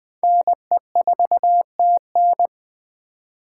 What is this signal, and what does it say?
Morse code, heard as a keyed tone, about 20 wpm, 700 Hz. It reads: NE4TN
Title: 2023 SOTA Chaser Call Signs - 20wpm